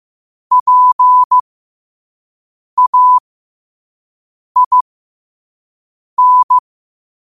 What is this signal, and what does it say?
Morse code, heard as a keyed tone, about 15 wpm, spaced as for 5 wpm, 1000 Hz. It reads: PAIN
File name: 5wpm